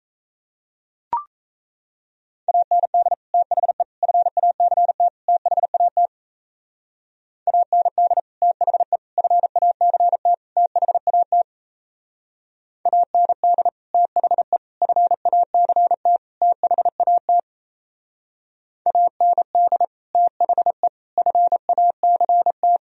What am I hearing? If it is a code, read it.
Morse code, heard as a keyed tone, about 36 wpm, 700 Hz. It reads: AND THE FACT THAT AND THE FACT THAT AND THE FACT THAT AND THE FACT